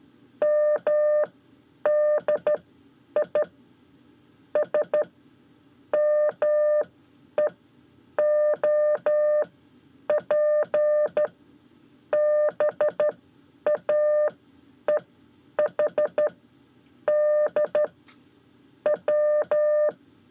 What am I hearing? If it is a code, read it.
Morse code, heard as a keyed tone, about 13 wpm, spaced as for 9 wpm, 600 Hz. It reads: MDI S MEOP BAEHD W
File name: 5 Lezione CW 1